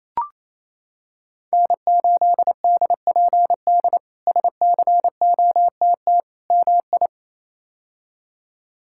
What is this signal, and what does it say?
Morse code, heard as a keyed tone, about 28 wpm, 700 Hz. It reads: N8DPD SCOTT MI
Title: K1USN SST QSO Practice - 28wpm